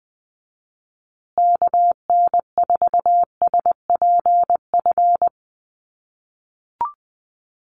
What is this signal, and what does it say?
Morse code, heard as a keyed tone, about 20 wpm, 700 Hz. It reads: KN4SPF
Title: USA Call Signs Repeated 3x 20wpm